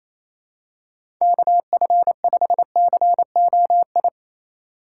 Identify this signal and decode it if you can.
Morse code, heard as a keyed tone, about 28 wpm, 700 Hz. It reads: KF5COI
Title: USA Call Signs - 6 Characters - 4x Character Spacing - 28wpm